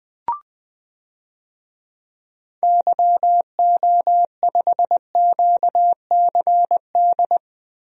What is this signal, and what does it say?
Morse code, heard as a keyed tone, about 20 wpm, 700 Hz. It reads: YO5QCD
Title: DXCC-100 Call Signs - 6 Characters - Repeated 2x - 20wpm